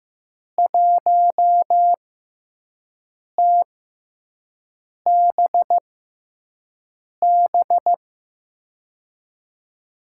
Morse code, heard as a keyed tone, about 15 wpm, spaced as for 5 wpm, 700 Hz. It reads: 1TBB
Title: USA Call Signs - 5 Characters - 6x Character Spacing - 15wpm